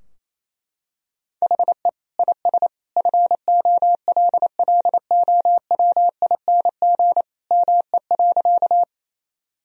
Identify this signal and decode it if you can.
Morse code, heard as a keyed tone, about 28 wpm, 700 Hz. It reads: HE IS FOLLOWING ME.
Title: Sentences from Top 1000 Words - Review Part 4 - 28wpm